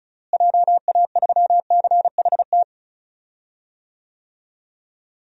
Morse code, heard as a keyed tone, about 36 wpm, 700 Hz. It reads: JA3CHT
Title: DXCC-100 Call Signs - 6 Characters - Repeated 3x - 35wpm